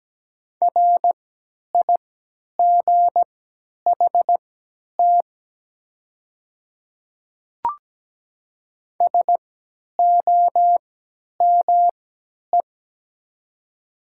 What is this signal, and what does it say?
Morse code, heard as a keyed tone, about 17 wpm, spaced as for 10 wpm, 690 Hz. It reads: RIGHT SOME
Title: 3x Character Spacing - Top 300 Words - 17wpm